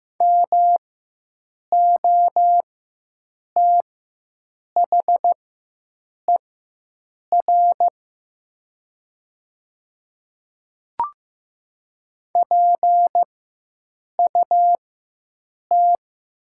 Morse code, heard as a keyed tone, about 15 wpm, spaced as for 7 wpm, 700 Hz. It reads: MOTHER PUT
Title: 4x Character Spacing - Top 300 Words - 15wpm